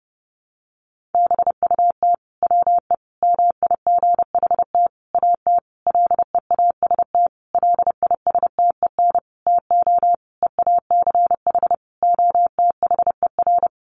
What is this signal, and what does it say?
Morse code, heard as a keyed tone, about 30 wpm, 700 Hz. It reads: BUT WE MIGHT AT LEAST LISTEN TO EACH OTHER
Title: Sentences from Top 1300 Words 30wpm